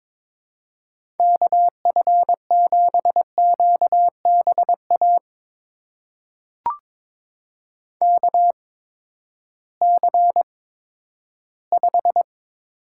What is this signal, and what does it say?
Morse code, heard as a keyed tone, about 22 wpm, spaced as for 12 wpm, 700 Hz. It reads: KF7QBA KC5
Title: USA Call Signs - 6 Characters - 8x Character Spacing - 22wpm